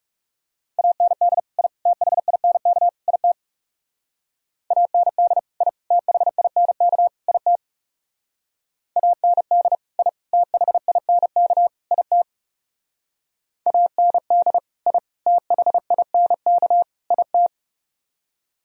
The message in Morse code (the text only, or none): AND I THINK IT AND I THINK IT AND I THINK IT AND I THINK IT